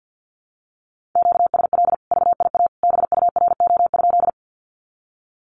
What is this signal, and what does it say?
Morse code, heard as a keyed tone, about 48 wpm, 700 Hz. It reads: QSL VIA BURO?
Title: Common QSO Phrases 50wpm